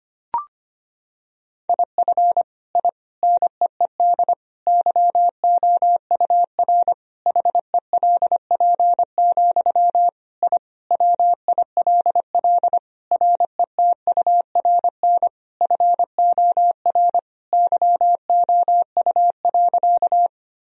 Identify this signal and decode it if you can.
Morse code, heard as a keyed tone, about 25 wpm, 700 Hz. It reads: IF I NEED YOUR HELP, I WILL RETURN FOR YOU.
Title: Sentences from Top 1000 Words - Review Part 1 - 25wpm